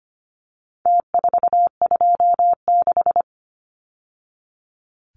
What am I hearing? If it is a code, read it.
Morse code, heard as a keyed tone, about 25 wpm, 700 Hz. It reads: T426
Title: Numbers - Contest Style - 4-Digit 25wpm